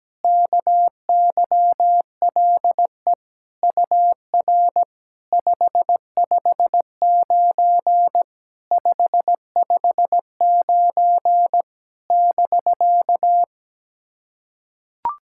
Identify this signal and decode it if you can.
Morse code, heard as a keyed tone, about 17 wpm, 700 Hz. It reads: KYLE UR 559 559 <BK>
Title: SOTA QSO Practice 17wpm